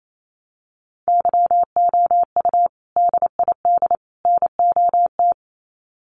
Morse code, heard as a keyed tone, about 28 wpm, 700 Hz. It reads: YOU DID NOT
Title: Sets of 3 Words 28wpm